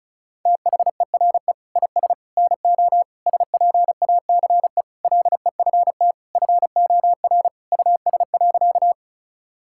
Morse code, heard as a keyed tone, about 36 wpm, 700 Hz. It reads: THERE IS NO SPACE LEFT FOR US.
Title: Sentences from Top 1000 Words - Review Part 5 - 35wpm